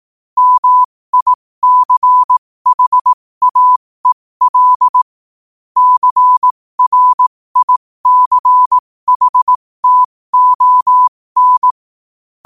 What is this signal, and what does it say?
Morse code, heard as a keyed tone, about 18 wpm, spaced as for 15 wpm, 1000 Hz. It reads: MICHAEL CRICHTON
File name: qotdfiles / 15wpm